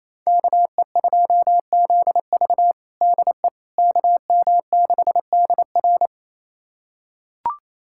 Morse code, heard as a keyed tone, about 28 wpm, 700 Hz. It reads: KE2ZV DE KM6DR
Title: USA Callsign DE Callsign - 5 Characters - 28wpm